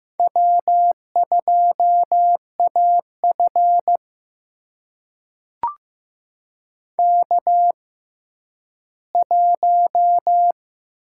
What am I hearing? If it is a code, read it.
Morse code, heard as a keyed tone, about 15 wpm, spaced as for 10 wpm, 700 Hz. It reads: W2AF K1
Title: USA Call Signs - 4 Characters - 6x Character Spacing - 15wpm